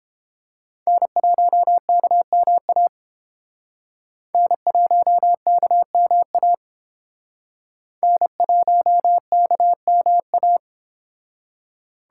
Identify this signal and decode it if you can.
Morse code, heard as a keyed tone, about 29 wpm, 700 Hz. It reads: N1KMA N1KMA N1KMA